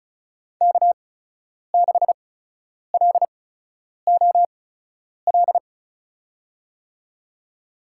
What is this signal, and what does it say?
Morse code, heard as a keyed tone, about 34 wpm, spaced as for 10 wpm, 700 Hz. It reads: K6LOL
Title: USA Call Signs - 5 Characters - 8x Character Spacing - 35wpm